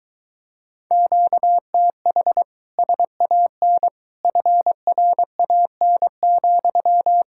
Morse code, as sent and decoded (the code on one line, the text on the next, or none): --.- - .... / ... .- -. / ..-. .-. .- -. --..--
QTH SAN FRAN,